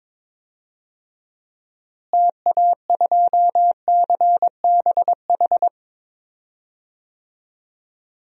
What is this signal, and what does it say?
Morse code, heard as a keyed tone, about 23 wpm, 700 Hz. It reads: TA2CBH
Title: DXCC-100 Call Signs - 6 Characters - Repeated 2x - 22wpm